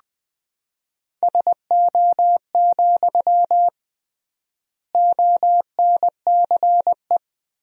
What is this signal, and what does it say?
Morse code, heard as a keyed tone, about 20 wpm, 700 Hz. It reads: SO, ONCE
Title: 3x Word-Spacing - Sentences from Top 300 Words - 20wpm